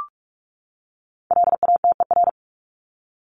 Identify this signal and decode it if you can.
Morse code, heard as a keyed tone, about 44 wpm, 700 Hz. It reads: LATER
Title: Top 501-750 Words 45wpm